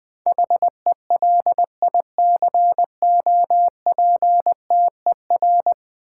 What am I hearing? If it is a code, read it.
Morse code, heard as a keyed tone, about 20 wpm, 700 Hz. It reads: HELICOPTER
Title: Build A Word - 5 Letters At A Time - 10-Letter Words - 20wpm